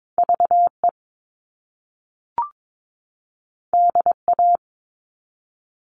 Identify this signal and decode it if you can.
Morse code, heard as a keyed tone, about 22 wpm, 700 Hz. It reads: VE DA
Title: All Cut Numbers - 2-Digit 22wpm